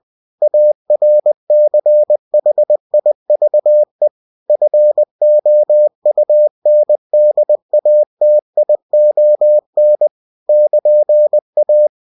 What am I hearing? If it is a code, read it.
Morse code, heard as a keyed tone, about 20 wpm, 600 Hz. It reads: ARCHIVE FOUNDATION <KN>A
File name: cw-books / winnie-the-pooh / winnie-20wpm-page-0114